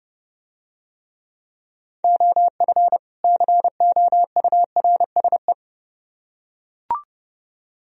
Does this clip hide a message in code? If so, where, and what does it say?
Yes: Morse code from 2.0 to 5.5 s, heard as a keyed tone, about 30 wpm, 700 Hz. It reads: OF COURSE